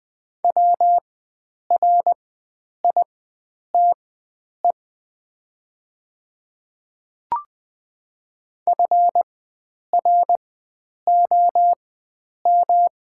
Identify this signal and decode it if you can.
Morse code, heard as a keyed tone, about 20 wpm, spaced as for 9 wpm, 700 Hz. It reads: WRITE FROM
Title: 4x Character Spacing - Top 100 Words - 20wpm